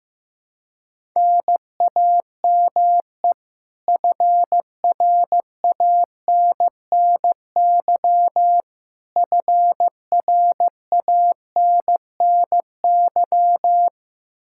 Morse code, heard as a keyed tone, about 15 wpm, 700 Hz. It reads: NAME FRANNY FRANNY
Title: Practice QSO - Exchange 1 - Beginning - 15wpm